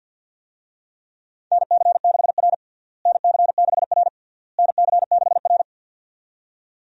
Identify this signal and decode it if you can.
Morse code, heard as a keyed tone, about 52 wpm, 700 Hz. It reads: NC6R NC6R NC6R
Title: USA Call Signs - 4 Characters - Repeated 3x - 50wpm